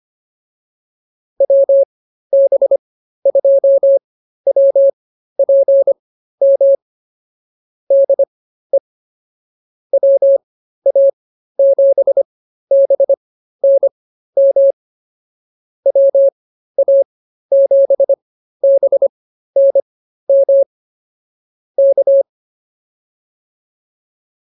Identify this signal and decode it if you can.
Morse code, heard as a keyed tone, about 24 wpm, spaced as for 13 wpm, 550 Hz. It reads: WB2WPM DE WA7BNM WA7BNM K
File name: QSO106 13